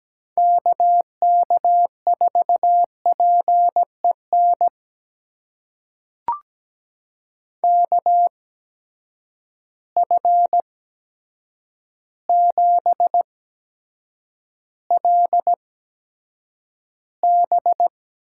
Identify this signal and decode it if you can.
Morse code, heard as a keyed tone, about 17 wpm, spaced as for 8 wpm, 700 Hz. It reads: KK4PEN KF7LB
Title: USA Call Signs - 6 Characters - 8x Character Spacing - 17wpm